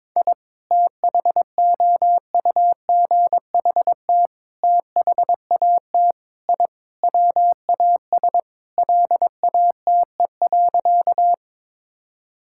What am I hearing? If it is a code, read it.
Morse code, heard as a keyed tone, about 22 wpm, 700 Hz. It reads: I THOUGHT THAT I WAS LATE.
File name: practice / 22wpm / Sentences with 3-Word Phrases - 22wpm